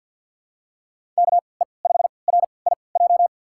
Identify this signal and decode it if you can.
Morse code, heard as a keyed tone, about 47 wpm, spaced as for 28 wpm, 700 Hz. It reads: KE5RIJ
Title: USA Call Signs - 6 Characters - 3x Character Spacing - 50wpm